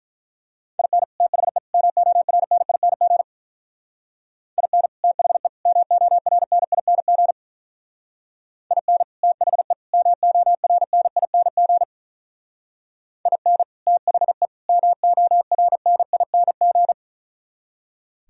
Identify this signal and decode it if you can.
Morse code, heard as a keyed tone, about 43 wpm, 700 Hz. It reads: IN THE MORNING IN THE MORNING IN THE MORNING IN THE MORNING
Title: Speed-Racing - Sets of 3 Words 35wpm